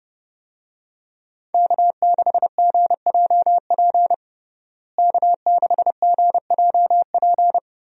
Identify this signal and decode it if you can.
Morse code, heard as a keyed tone, about 30 wpm, 700 Hz. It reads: K6GJP K6GJP
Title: USA Call Signs - 5 Characters - Repeated 2x - 30wpm